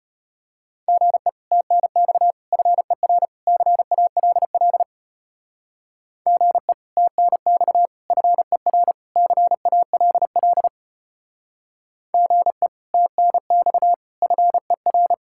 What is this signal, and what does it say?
Morse code, heard as a keyed tone, about 30 wpm, 700 Hz. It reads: GE TNX FER CALL GE TNX FER CALL GE TNX FER
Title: Speed-Racing Common QSO Phrases Encore - 25wpm